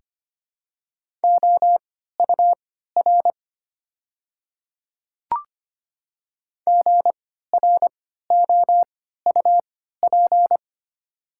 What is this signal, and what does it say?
Morse code, heard as a keyed tone, about 25 wpm, spaced as for 14 wpm, 690 Hz. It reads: OUR GROUP
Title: 3x Character Spacing - Top 300 Words - 25wpm